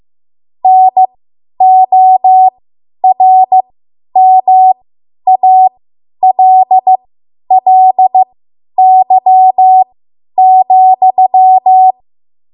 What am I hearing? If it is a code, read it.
Morse code, heard as a keyed tone, about 15 wpm, spaced as for 10 wpm, 750 Hz. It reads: NORMALLY,
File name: Archive / 10 WPM / 10WPM